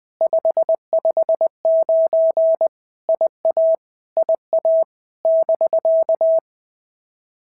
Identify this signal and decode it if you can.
Morse code, heard as a keyed tone, about 20 wpm, 650 Hz. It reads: 559 IA IA <BK>
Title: POTA QSO Practice - 20wpm